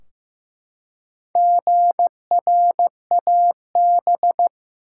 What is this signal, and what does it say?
Morse code, heard as a keyed tone, about 15 wpm, 700 Hz. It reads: GRAB
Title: Four Letter Words - 15wpm